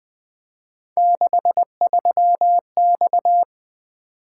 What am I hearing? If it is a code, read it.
Morse code, heard as a keyed tone, about 23 wpm, 700 Hz. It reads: 63X
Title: Speed-Racing - Three Letter-Number Combinations - 20wpm